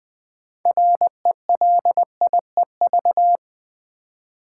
Morse code, heard as a keyed tone, about 20 wpm, 700 Hz. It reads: RELIEV
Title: Build A Word - 3 Letters At A Time - 8-Letter Words - 20wpm